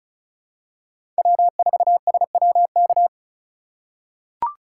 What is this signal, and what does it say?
Morse code, heard as a keyed tone, about 36 wpm, 700 Hz. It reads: W4SWK
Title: USA Call Signs - 5 Characters - 3x Character Spacing - 35wpm